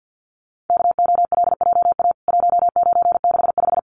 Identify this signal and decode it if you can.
Morse code, heard as a keyed tone, about 48 wpm, 700 Hz. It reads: KOLJA 1965
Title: CWT Exchange 50wpm